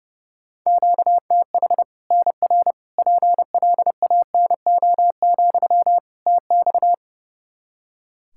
Morse code, heard as a keyed tone, about 32 wpm, 700 Hz. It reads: QTH NR PLANO, TX
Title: Practice QSO - Exchange 1 - Beginning - 30wpm